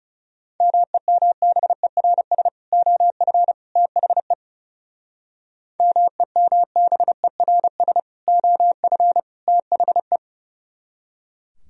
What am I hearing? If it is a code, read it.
Morse code, heard as a keyed tone, about 37 wpm, 700 Hz. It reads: MEMBERS OF THE MEMBERS OF THE
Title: Speed-Racing - Sets of 3 Words 30wpm